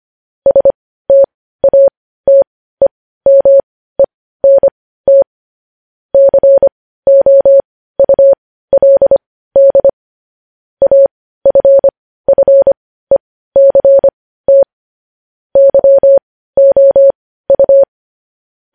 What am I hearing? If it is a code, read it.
Morse code, heard as a keyed tone, about 25 wpm, spaced as for 15 wpm, 550 Hz. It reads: STATEMENT COULD AFFECT YOU
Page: News Headlines in Morse Code at 15 WPM